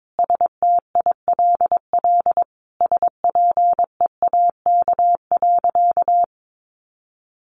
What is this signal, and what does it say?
Morse code, heard as a keyed tone, about 22 wpm, 700 Hz. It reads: STILL SPEAK.
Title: Sentences from Top 1100 Words 22wpm